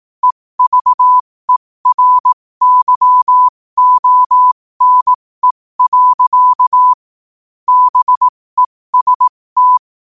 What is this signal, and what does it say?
Morse code, heard as a keyed tone, about 18 wpm, spaced as for 15 wpm, 1000 Hz. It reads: EVERYONE. BEST
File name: qotdfiles / 15wpm